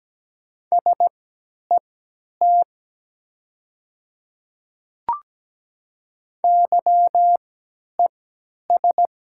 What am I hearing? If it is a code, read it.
Morse code, heard as a keyed tone, about 17 wpm, spaced as for 10 wpm, 700 Hz. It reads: SET YES
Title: 3x Character Spacing - Top 500 Words - 17wpm